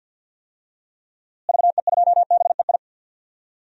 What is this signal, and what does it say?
Morse code, heard as a keyed tone, about 47 wpm, 700 Hz. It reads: VE2BEI
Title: DXCC-100 Call Signs - 6 Characters - Repeated 3x - 50wpm